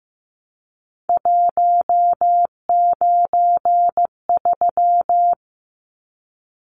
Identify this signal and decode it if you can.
Morse code, heard as a keyed tone, about 15 wpm, 700 Hz. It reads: 193
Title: Numbers - 3-Digit 15wpm